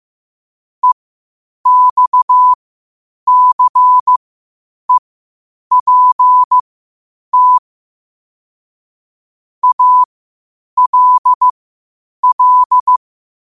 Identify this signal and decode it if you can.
Morse code, heard as a keyed tone, about 15 wpm, spaced as for 8 wpm, 1000 Hz. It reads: EXCEPT ALL
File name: qotdfiles / 8wpm